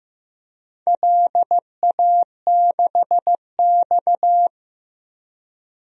Morse code, heard as a keyed tone, about 15 wpm, 700 Hz. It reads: LA6X
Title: DXCC-100 Call Signs - 4 Characters - 15wpm